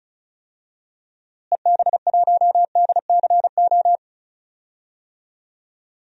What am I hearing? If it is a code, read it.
Morse code, heard as a keyed tone, about 36 wpm, 700 Hz. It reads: EB1DCO
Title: DXCC-100 Call Signs - 6 Characters - Repeated 3x - 35wpm